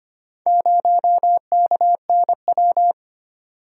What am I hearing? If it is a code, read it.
Morse code, heard as a keyed tone, about 25 wpm, 690 Hz. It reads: 0KNW
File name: practice / 25wpm / Four Letter-Number Combinations - 25wpm